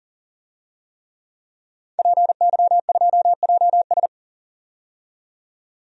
Morse code, heard as a keyed tone, about 40 wpm, 700 Hz. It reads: PY2JS
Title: DXCC-100 Call Signs - 5 Characters - 40wpm